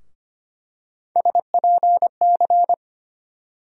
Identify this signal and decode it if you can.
Morse code, heard as a keyed tone, about 25 wpm, 700 Hz. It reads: SPC
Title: USA Call Sign Postfixes - 3 Letters 25wpm